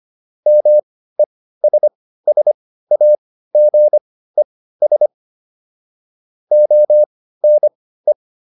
Morse code, heard as a keyed tone, about 25 wpm, spaced as for 15 wpm, 600 Hz. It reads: MESSAGES ONE